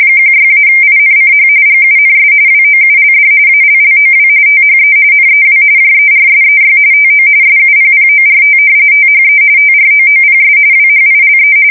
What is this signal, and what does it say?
RTTY, about 45.45 baud. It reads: 16-INCH GUNS WOULD ALLOW THE NEW JERSEY TO SUPPORT THE TROOPS WHILE
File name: fsk170 / battleshipnj